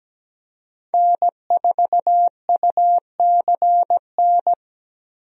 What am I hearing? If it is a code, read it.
Morse code, heard as a keyed tone, about 17 wpm, 700 Hz. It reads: N4UCN
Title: USA Call Signs - 5 Characters - Repeated 3x - 17wpm